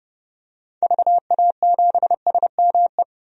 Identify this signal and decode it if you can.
Morse code, heard as a keyed tone, about 29 wpm, 700 Hz. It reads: VA7SME
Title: DXCC-100 Call Signs - 6 Characters - 30wpm